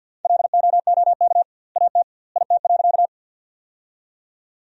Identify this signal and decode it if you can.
Morse code, heard as a keyed tone, about 49 wpm, 700 Hz. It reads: LOOK AT IT.
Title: 2x Word-Spacing - Sentences from Top 100 Words - 50wpm